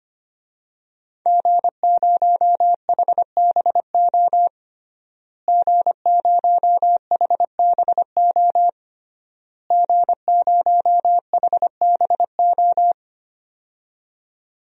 Morse code, heard as a keyed tone, about 25 wpm, 700 Hz. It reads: G0HBO G0HBO G0HBO
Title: DXCC-100 Call Signs - 5 Characters - Repeated 3x - 25wpm